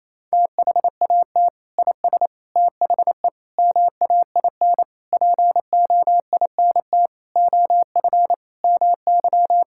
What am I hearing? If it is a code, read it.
Morse code, heard as a keyed tone, about 28 wpm, 700 Hz. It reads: THAT IS THE MAIN POINT OF MY
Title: Sentences from Top 1000 Words - Review Part 5 - 28wpm